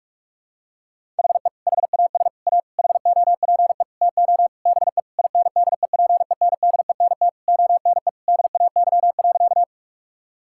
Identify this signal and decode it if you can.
Morse code, heard as a keyed tone, about 46 wpm, 700 Hz. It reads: HE HAS A HOPE TO BE INDEPENDENT ONE DAY.
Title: Sentences from Top 1000 Words - Review Part 6 - 45wpm